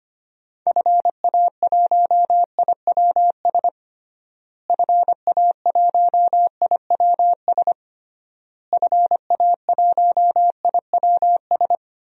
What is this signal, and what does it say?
Morse code, heard as a keyed tone, about 25 wpm, 700 Hz. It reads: FA1IWS FA1IWS FA1IWS
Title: DXCC-100 Call Signs - 6 Characters - Repeated 3x - 25wpm